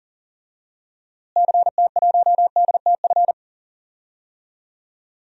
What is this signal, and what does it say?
Morse code, heard as a keyed tone, about 40 wpm, 700 Hz. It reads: CT1DTF
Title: DXCC-100 Call Signs - 6 Characters - Repeated 2x - 40wpm